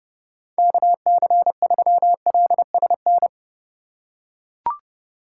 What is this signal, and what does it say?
Morse code, heard as a keyed tone, about 28 wpm, 700 Hz. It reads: KC3LSN